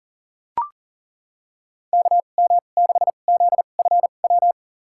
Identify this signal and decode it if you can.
Morse code, heard as a keyed tone, about 39 wpm, spaced as for 28 wpm, 700 Hz. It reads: KM6ZFW
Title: USA Call Signs - 6 Characters - 2x Character Spacing - 40wpm